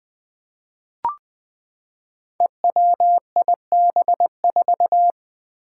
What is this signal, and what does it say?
Morse code, heard as a keyed tone, about 20 wpm, 700 Hz. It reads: EWIB4
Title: Five Challenging Letter-Number Combinations 20wpm